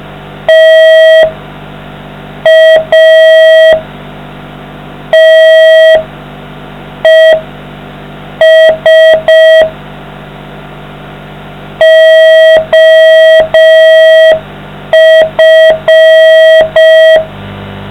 Morse code, heard as a keyed tone, about 5 wpm, spaced as for 4 wpm, 640 Hz. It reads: TATES OF